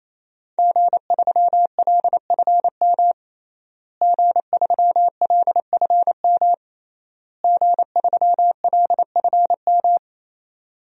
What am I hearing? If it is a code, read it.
Morse code, heard as a keyed tone, about 28 wpm, 700 Hz. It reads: G3LFM G3LFM G3LFM
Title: DXCC-100 Call Signs - 5 Characters - Repeated 3x - 28wpm